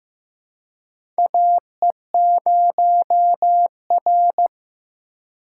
Morse code, heard as a keyed tone, about 15 wpm, 700 Hz. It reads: AE0R
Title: USA Call Signs - 4 Characters - 4x Character Spacing - 15wpm